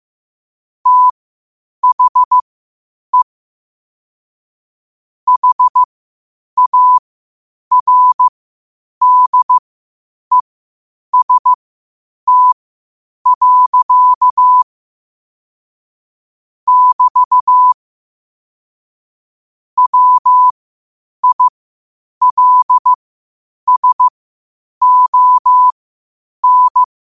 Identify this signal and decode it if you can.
Morse code, heard as a keyed tone, about 15 wpm, spaced as for 8 wpm, 1000 Hz. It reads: THE HARDEST. <BT> WILSON
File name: qotdfiles / 8wpm